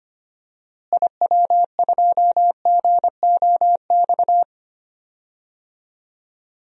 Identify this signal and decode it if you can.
Morse code, heard as a keyed tone, about 25 wpm, 700 Hz. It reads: IW2GOX
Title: DXCC-100 Call Signs - 6 Characters - Repeated 2x - 25wpm